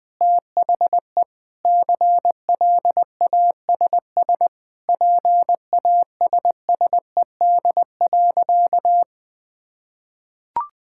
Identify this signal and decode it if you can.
Morse code, heard as a keyed tone, about 20 wpm, 700 Hz. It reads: THE CLASS PASSED.
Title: Sentences with 5-Word Phrases - 20wpm